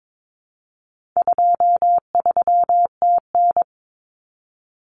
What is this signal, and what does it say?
Morse code, heard as a keyed tone, about 22 wpm, 700 Hz. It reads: 23TN
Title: Common Cut Numbers - 4-Digit 22wpm